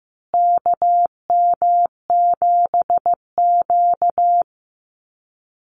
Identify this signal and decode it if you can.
Morse code, heard as a keyed tone, about 15 wpm, 700 Hz. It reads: KM7Q
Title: Call Signs Repeated 2x - 15wpm